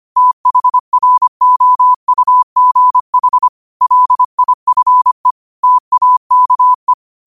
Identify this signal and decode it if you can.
Morse code, heard as a keyed tone, about 25 wpm, 1000 Hz. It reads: THROUGH LIFE TAKE